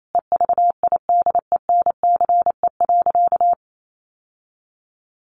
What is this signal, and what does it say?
Morse code, heard as a keyed tone, about 28 wpm, 700 Hz. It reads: EVIDENCE.
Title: Sentences from Top 1300 Words 28wpm